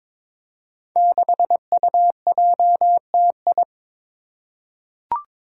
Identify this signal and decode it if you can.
Morse code, heard as a keyed tone, about 22 wpm, 700 Hz. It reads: 6UJTI
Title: Five Challenging Letter-Number Combinations 22wpm